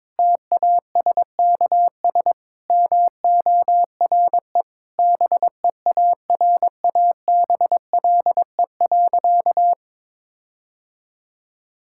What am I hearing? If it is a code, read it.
Morse code, heard as a keyed tone, about 22 wpm, 700 Hz. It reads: TASKS MORE BEARABLE.
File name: Difficult Sentences with Hints - 22wpm